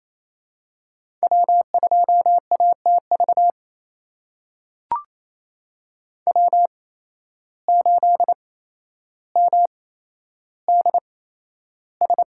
Morse code, heard as a keyed tone, about 28 wpm, spaced as for 12 wpm, 700 Hz. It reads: W2ATV W8MDS
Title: USA Call Signs - 8x Character Spacing - 28wpm